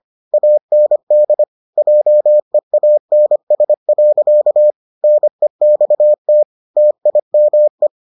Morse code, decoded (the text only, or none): AND JEANS. NEXT TIME